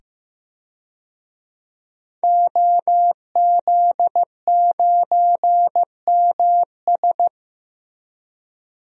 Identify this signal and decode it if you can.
Morse code, heard as a keyed tone, about 15 wpm, 700 Hz. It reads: OZ9MS
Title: DXCC-100 Call Signs - 5 Characters - 15wpm